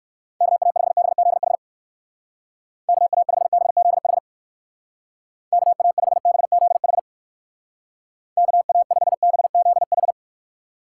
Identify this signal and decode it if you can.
Morse code, heard as a keyed tone, about 57 wpm, 700 Hz. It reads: KA5BZH KA5BZH KA5BZH KA5BZH